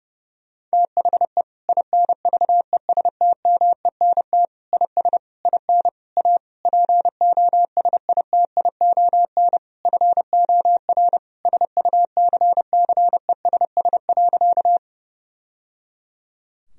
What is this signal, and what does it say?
Morse code, heard as a keyed tone, about 29 wpm, 700 Hz. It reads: THE INVESTMENT IS IN A POSITION FOR SUCCESS.
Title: Sentences from Top 1000 Words - Review Part 4 - 30wpm